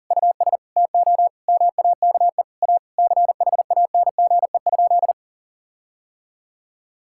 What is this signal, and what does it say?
Morse code, heard as a keyed tone, about 37 wpm, 700 Hz. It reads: US TO MAKE A CHANGE?
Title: Sentences from Top 1600 Words 40wpm